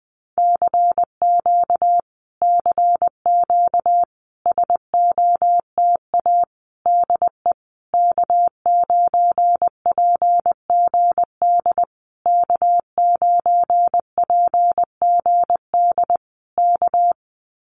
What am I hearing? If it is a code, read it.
Morse code, heard as a keyed tone, about 20 wpm, 700 Hz. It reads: CQ CQ SOTA DE K9PGD K9PGD K
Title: Common QSO Phrases Encore 20wpm